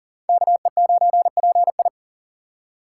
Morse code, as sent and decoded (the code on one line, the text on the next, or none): -.- . ----. .--. ..
KE9PI